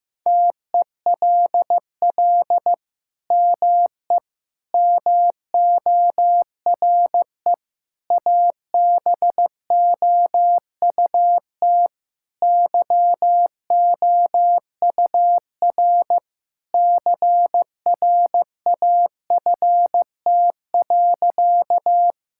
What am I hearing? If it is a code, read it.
Morse code, heard as a keyed tone, about 15 wpm, 700 Hz. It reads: TELL ME MORE ABOUT YOUR CRAFT.
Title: Sentences from Top 1000 Words - Review Part 6 - 15wpm